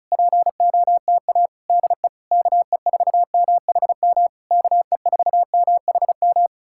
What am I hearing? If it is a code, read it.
Morse code, heard as a keyed tone, about 37 wpm, 700 Hz. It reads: POTA DE KE4MHM KE4MHM
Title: Speed-Racing Common QSO Phrases Encore - 30wpm